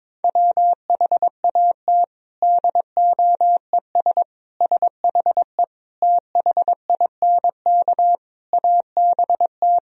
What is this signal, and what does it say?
Morse code, heard as a keyed tone, about 22 wpm, 700 Hz. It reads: WHAT DOES SHE THINK ABT
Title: Sentences from Top 200 Words and Abbreviations 22wpm